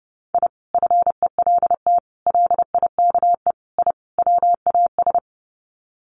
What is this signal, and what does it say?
Morse code, heard as a keyed tone, about 30 wpm, 700 Hz. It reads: I FELT LIKE I WAS
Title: Sets of 5 Words Encore 30wpm